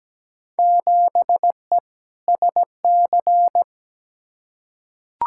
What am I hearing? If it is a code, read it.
Morse code, heard as a keyed tone, about 17 wpm, 700 Hz. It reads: 7E SC
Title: Field Day Exchanges Encore - 17wpm